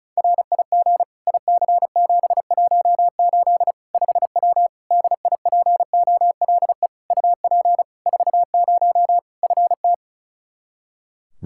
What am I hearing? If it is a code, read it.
Morse code, heard as a keyed tone, about 36 wpm, 700 Hz. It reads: RIG IC718 5W DIPOLE UP 40 FT
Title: K1USN SST QSO Practice 35wpm